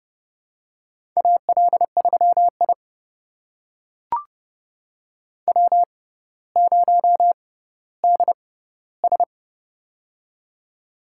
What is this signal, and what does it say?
Morse code, heard as a keyed tone, about 32 wpm, spaced as for 16 wpm, 700 Hz. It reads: AL3I W0DS
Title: USA Call Signs - 6x Character Spacing - 30wpm